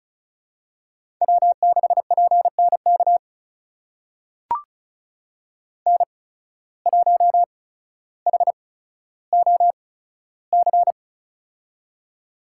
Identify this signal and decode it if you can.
Morse code, heard as a keyed tone, about 36 wpm, spaced as for 15 wpm, 700 Hz. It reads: W6PNK N1HOC
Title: USA Call Signs - 5 Characters - 8x Character Spacing - 35wpm